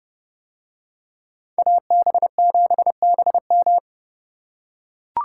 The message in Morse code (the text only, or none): AB7BM